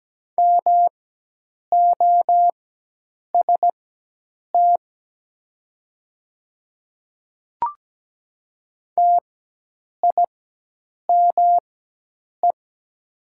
Morse code, heard as a keyed tone, about 17 wpm, spaced as for 8 wpm, 700 Hz. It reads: MOST TIME
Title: 4x Character Spacing - Top 300 Words - 17wpm